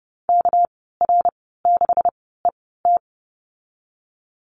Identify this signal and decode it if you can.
Morse code, heard as a keyed tone, about 30 wpm, spaced as for 17 wpm, 700 Hz. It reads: KR6ET
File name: Lesson 37g - Letter x - Call Signs 30wpm